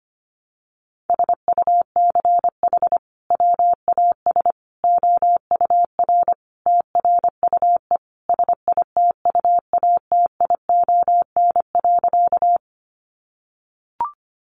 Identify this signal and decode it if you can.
Morse code, heard as a keyed tone, about 25 wpm, 700 Hz. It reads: SUCH WAS OUR TRUE SITUATION.
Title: Sentences from Top 1200 Words 25wpm